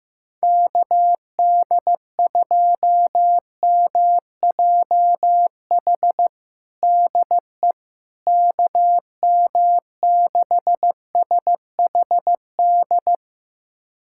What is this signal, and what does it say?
Morse code, heard as a keyed tone, about 15 wpm, 700 Hz. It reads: KD2MJH DE KM6SHD
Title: USA Callsign DE Callsign 15wpm